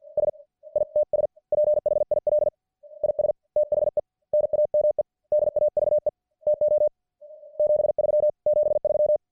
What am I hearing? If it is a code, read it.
Morse code, heard as a keyed tone, about 50 wpm, 590 Hz. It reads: S ITS PHIL IS THE NAME DAVE TO 73 73